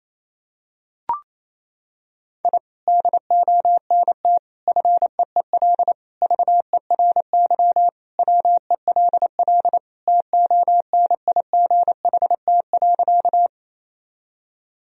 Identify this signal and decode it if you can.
Morse code, heard as a keyed tone, about 28 wpm, 700 Hz. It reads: I DONT FEEL VERY WELL TONIGHT.
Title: Sentences from Top 1000 Words - Review Part 3 - 28wpm